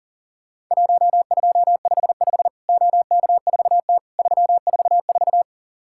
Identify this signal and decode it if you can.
Morse code, heard as a keyed tone, about 40 wpm, 700 Hz. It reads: 1255 OK4T 344
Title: Worked All Europe DX Contest - QTC Exchange - 40wpm